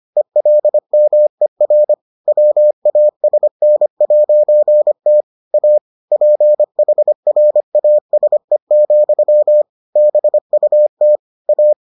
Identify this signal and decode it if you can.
Morse code, heard as a keyed tone, about 25 wpm, 600 Hz. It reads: ELMER WASN'T A PHRASE, BUT A